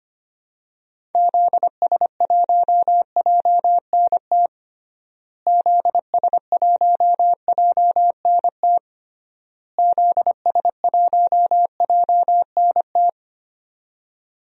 Morse code, heard as a keyed tone, about 25 wpm, 700 Hz. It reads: ZS1JNT ZS1JNT ZS1JNT
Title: DXCC-100 Call Signs - 6 Characters - Repeated 3x - 25wpm